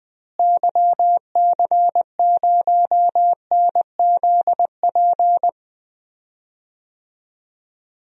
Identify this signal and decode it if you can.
Morse code, heard as a keyed tone, about 21 wpm, 700 Hz. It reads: YC0NZP
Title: DXCC-100 Call Signs - 6 Characters - Repeated 2x - 20wpm